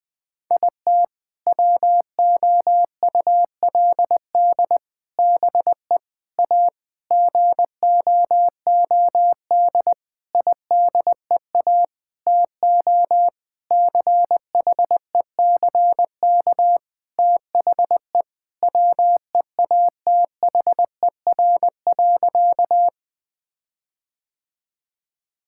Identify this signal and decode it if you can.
Morse code, heard as a keyed tone, about 20 wpm, 700 Hz. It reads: IT WOULD BE A GOOD IDEA TO CHECK THE WEATHER.
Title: Sentences with 6-Word Phrases - 20wpm